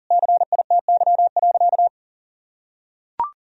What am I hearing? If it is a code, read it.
Morse code, heard as a keyed tone, about 40 wpm, 700 Hz. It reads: CITY.
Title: Sentences from Top 1000 Words - Review Part 1 - 40wpm